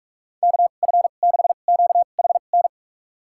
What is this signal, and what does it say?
Morse code, heard as a keyed tone, about 47 wpm, spaced as for 33 wpm, 700 Hz. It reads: KF6QHN
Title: USA Call Signs - 2x Character Spacing - 45wpm